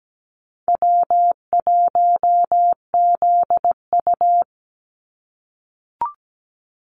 Morse code, heard as a keyed tone, about 17 wpm, 700 Hz. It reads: W1ZU
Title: NA SOTA Activator Callsigns 17wpm